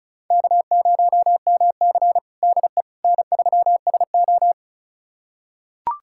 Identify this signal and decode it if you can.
Morse code, heard as a keyed tone, about 36 wpm, 700 Hz. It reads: K0MC DE N3SO
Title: USA Callsign DE Callsign - 4 Characters - 35wpm